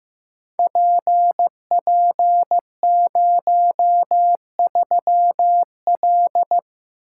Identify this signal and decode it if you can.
Morse code, heard as a keyed tone, about 15 wpm, 700 Hz. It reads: PP03L
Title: Code Groups - 5 Letter:Numbers - 15wpm